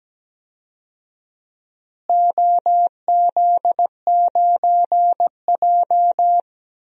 Morse code, heard as a keyed tone, about 17 wpm, 700 Hz. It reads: OZ9J
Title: DXCC-100 Call Signs - 4 Characters - Repeated 2x - 17wpm